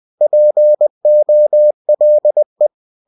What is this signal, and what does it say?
Morse code, heard as a keyed tone, about 20 wpm, 600 Hz. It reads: POLE
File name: cw-books / winnie-the-pooh / winnie-20wpm-page-0093